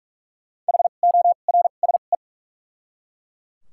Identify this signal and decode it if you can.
Morse code, heard as a keyed tone, about 42 wpm, spaced as for 31 wpm, 700 Hz. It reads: HORSE
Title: 2x Character Spacing - Top 300 Words - 45wpm